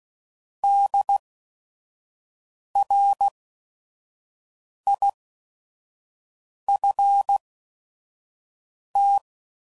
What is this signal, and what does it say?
Morse code, heard as a keyed tone, about 16 wpm, spaced as for 5 wpm, 800 Hz. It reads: DRIFT
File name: Prefixes / By First Letter / Ds